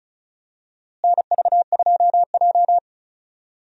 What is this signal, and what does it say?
Morse code, heard as a keyed tone, about 36 wpm, 700 Hz. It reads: NV2J
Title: USA Call Signs - 4 Characters - Repeated 3x - 35wpm